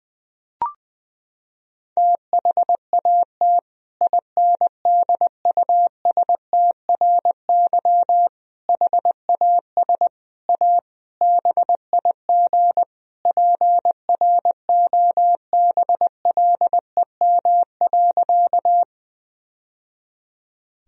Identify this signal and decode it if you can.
Morse code, heard as a keyed tone, about 20 wpm, 700 Hz. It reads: THAT INDUSTRY HAS A BIG PROBLEM.
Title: Sentences from Top 1000 Words - Review Part 4 - 20wpm